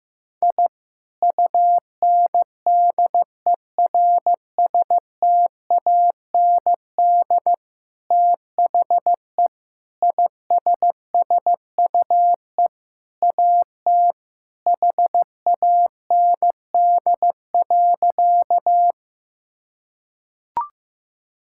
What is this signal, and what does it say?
Morse code, heard as a keyed tone, about 15 wpm, 700 Hz. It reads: I UNDERSTAND THE ISSUE AT HAND.
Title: Sentences from Top 1000 Words - Review Part 6 - 15wpm